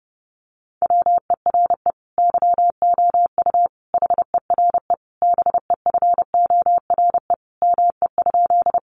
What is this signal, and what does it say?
Morse code, heard as a keyed tone, about 30 wpm, 700 Hz. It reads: WERE YOU HERE BEFORE ME?
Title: Sentences from Top 200 Words 30wpm